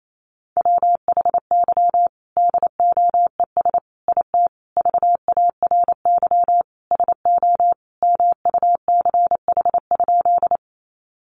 Morse code, heard as a keyed tone, about 28 wpm, 700 Hz. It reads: WHY DOES IT VARY SO MUCH?
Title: Sentences from Top 1200 Words 28wpm